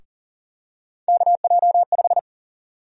Morse code, heard as a keyed tone, about 40 wpm, 700 Hz. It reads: KJ5
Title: USA Call Sign Prefixes - Repeated 2x - 40wpm